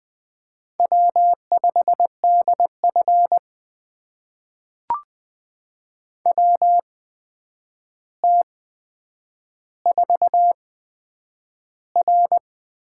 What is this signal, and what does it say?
Morse code, heard as a keyed tone, about 20 wpm, spaced as for 9 wpm, 700 Hz. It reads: W5DF WT4R
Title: USA Call Signs - 4 Characters - 8x Character Spacing - 20wpm